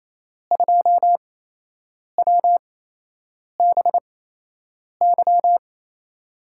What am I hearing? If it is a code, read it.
Morse code, heard as a keyed tone, about 27 wpm, spaced as for 8 wpm, 700 Hz. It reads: 2WBY
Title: USA Call Signs - 5 Characters - 8x Character Spacing - 28wpm